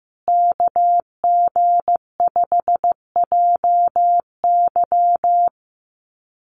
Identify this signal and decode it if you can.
Morse code, heard as a keyed tone, about 15 wpm, 700 Hz. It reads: KG5JY
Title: USA Call Signs Repeated 3x 15wpm